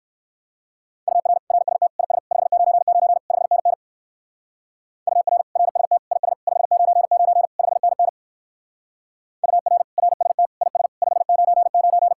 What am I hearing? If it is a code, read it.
Morse code, heard as a keyed tone, about 55 wpm, 700 Hz. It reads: UR RST IS 599 5NN UR RST IS 599 5NN UR RST IS 599
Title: Speed-Racing Common QSO Phrases - 45wpm